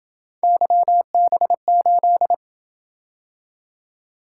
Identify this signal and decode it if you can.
Morse code, heard as a keyed tone, about 28 wpm, 700 Hz. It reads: YB8